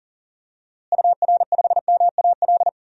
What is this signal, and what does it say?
Morse code, heard as a keyed tone, about 40 wpm, 700 Hz. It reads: UR5MAL
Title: DXCC-100 Call Signs - 6 Characters - Repeated 3x - 40wpm